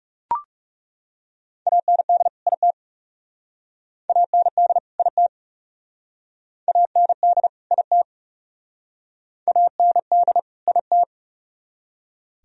Morse code, heard as a keyed tone, about 37 wpm, 700 Hz. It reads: AND IT AND IT AND IT AND IT
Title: Speed-Racing - Sets of 2 Words 30wpm